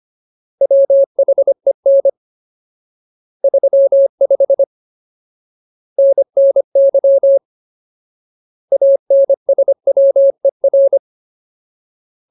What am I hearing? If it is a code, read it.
Morse code, heard as a keyed tone, about 25 wpm, 550 Hz. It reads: WHEN 35 NNY ANSWER
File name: Ultimate Word List / 25WPM Ultimate